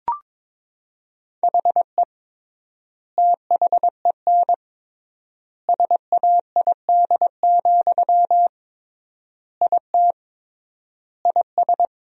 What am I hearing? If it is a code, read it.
Morse code, heard as a keyed tone, about 22 wpm, 700 Hz. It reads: HE THEN SAID, IT IS
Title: 3x Word-Spacing - Sentences from Top 300 Words - 22wpm